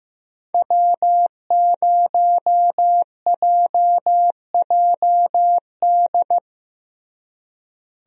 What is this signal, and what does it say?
Morse code, heard as a keyed tone, about 15 wpm, 700 Hz. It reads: W0JJD
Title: USA Call Signs - 5 Characters - Repeated 3x - 15wpm